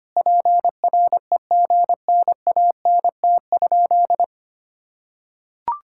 Morse code, heard as a keyed tone, about 25 wpm, 700 Hz. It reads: PREGNANT?
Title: Sentences from Top 1000 Words - Review Part 6 - 25wpm